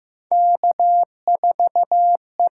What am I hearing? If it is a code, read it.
Morse code, heard as a keyed tone, about 15 wpm, 700 Hz. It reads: K4E
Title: USA Call Signs - 5 Characters - 8x Character Spacing - 15wpm